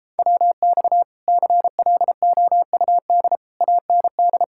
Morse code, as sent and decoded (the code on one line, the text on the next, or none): .-- -..- / -.-. .-.. --- ..- -.. / .- -. -..
WX CLOUD AND